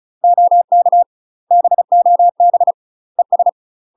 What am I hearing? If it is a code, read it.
Morse code, heard as a keyed tone, about 35 wpm, 700 Hz. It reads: OK BOB ES
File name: UK MASTER QSOs TRAINING / 35WPM / UK MASTER QSOs 0039